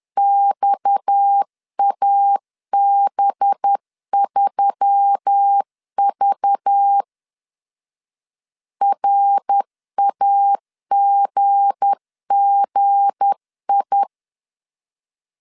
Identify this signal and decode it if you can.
Morse code, heard as a keyed tone, about 10 wpm, 800 Hz. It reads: XAB3V RAGGI
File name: LEZIONI / 7 GF9Z / 7 G 10wpm 2501